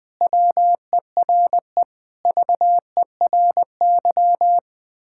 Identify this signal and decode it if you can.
Morse code, heard as a keyed tone, about 20 wpm, 700 Hz. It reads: WERE VERY
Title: Sentences with 6-Word Phrases - 20wpm